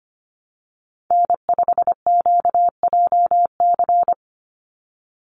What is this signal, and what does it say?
Morse code, heard as a keyed tone, about 25 wpm, 700 Hz. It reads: N5QJC
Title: USA Call Signs 25wpm